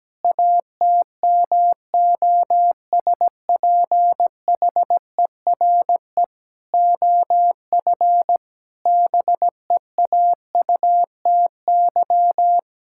Morse code, heard as a keyed tone, about 17 wpm, 700 Hz. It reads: ATMOSPHERE OF BEAUTY
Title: Sentences from Top 1600 Words 17wpm